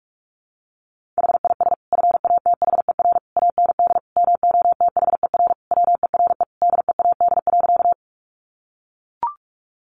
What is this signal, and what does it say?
Morse code, heard as a keyed tone, about 44 wpm, 700 Hz. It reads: HIS FATHER AND MOTHER WERE DEAD.
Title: Sentences from Top 1500 Words 45wpm